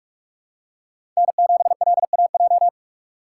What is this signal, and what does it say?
Morse code, heard as a keyed tone, about 46 wpm, 700 Hz. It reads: N7LAJ
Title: USA Call Signs - 5 Characters - 2x Character Spacing - 45wpm